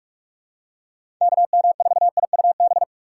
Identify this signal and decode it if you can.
Morse code, heard as a keyed tone, about 46 wpm, 700 Hz. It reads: KM4IUB
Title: USA Call Signs - 6 Characters - 4x Character Spacing - 45wpm